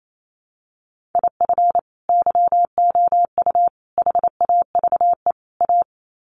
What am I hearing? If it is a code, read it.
Morse code, heard as a keyed tone, about 28 wpm, 700 Hz. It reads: IF YOU HAVE A
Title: Sets of 4 Words 28wpm